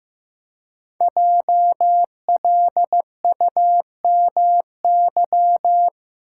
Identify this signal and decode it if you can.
Morse code, heard as a keyed tone, about 15 wpm, 700 Hz. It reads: JLUMY
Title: Code Groups - 5 Letters - 15wpm